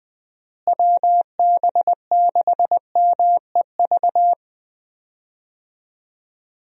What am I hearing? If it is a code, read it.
Morse code, heard as a keyed tone, about 20 wpm, 700 Hz. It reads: WB6MEV
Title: USA Call Signs - 6 Characters - Repeated 3x - 20wpm